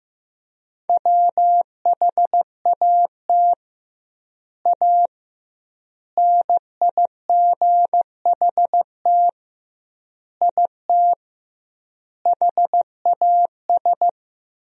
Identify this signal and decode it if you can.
Morse code, heard as a keyed tone, about 15 wpm, 700 Hz. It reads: WHAT A NIGHT IT HAS
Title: 2x Word-Spacing - Sentences from Top 300 Words - 15wpm